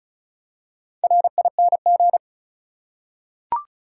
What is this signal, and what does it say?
Morse code, heard as a keyed tone, about 36 wpm, 700 Hz. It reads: RING
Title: Four Letter Words - 35wpm